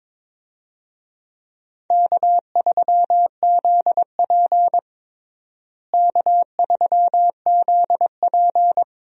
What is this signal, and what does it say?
Morse code, heard as a keyed tone, about 22 wpm, 700 Hz. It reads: K3ZP K3ZP
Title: USA Call Signs - 4 Characters - Repeated 2x - 22wpm